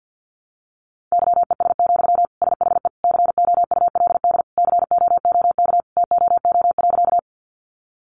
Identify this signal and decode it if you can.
Morse code, heard as a keyed tone, about 50 wpm, 700 Hz. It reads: YES, SHE COULD COOK TOO.
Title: Sentences from Top 700 Words 50wpm